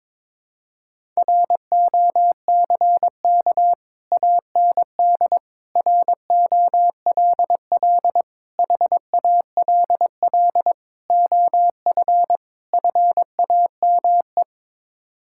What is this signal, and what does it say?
Morse code, heard as a keyed tone, about 22 wpm, 700 Hz. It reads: ROCK AND ROLL HALL OF FAME
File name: practice / 22wpm / Sets of 6 Words - 22wpm